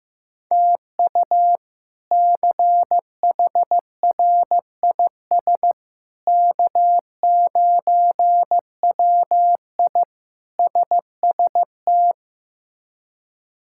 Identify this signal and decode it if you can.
Morse code, heard as a keyed tone, about 15 wpm, 700 Hz. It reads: TU CHRIS K9WI SST
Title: K1USN SST QSO Practice - 15wpm